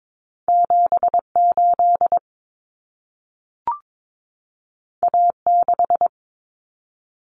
Morse code, heard as a keyed tone, about 22 wpm, 700 Hz. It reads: 78 A6
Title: Common Cut Numbers - 2-Digit 22wpm